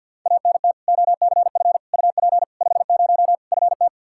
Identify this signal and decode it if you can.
Morse code, heard as a keyed tone, about 53 wpm, 700 Hz. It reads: ANT OCF UP 50 FT
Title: Practice QSO - Exchange 2 - Info - 50wpm